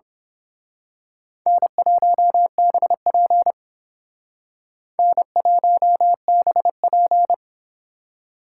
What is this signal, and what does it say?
Morse code, heard as a keyed tone, about 26 wpm, 700 Hz. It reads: N1BP N1BP
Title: Speed Racing - USA 1x2 Call Signs 22wpm